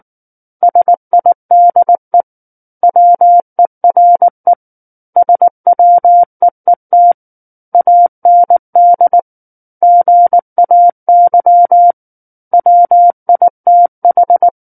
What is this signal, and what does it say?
Morse code, heard as a keyed tone, about 19 wpm, 700 Hz. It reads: SIDE WERE SWEET AND GAY WITH
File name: Book One / 10-Chapter 10 - 19 WPM